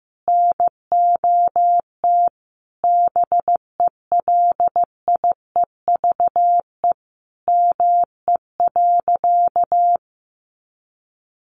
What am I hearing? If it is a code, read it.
Morse code, heard as a keyed tone, about 15 wpm, 700 Hz. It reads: NOT BELIEVE ME.
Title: Sentences from Top 1000 Words 15wpm